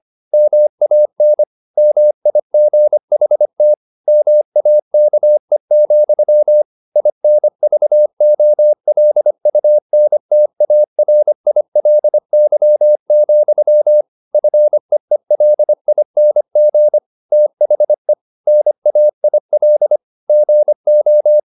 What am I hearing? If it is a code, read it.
Morse code, heard as a keyed tone, about 25 wpm, 600 Hz. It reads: MAN MIGHT MAKE, INVOLUNTARILY, FEELING THE NAIL GO